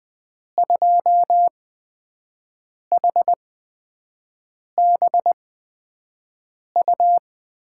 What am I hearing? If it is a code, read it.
Morse code, heard as a keyed tone, about 20 wpm, spaced as for 5 wpm, 700 Hz. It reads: 2HBU
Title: USA Call Signs - 6 Characters - 8x Character Spacing - 20wpm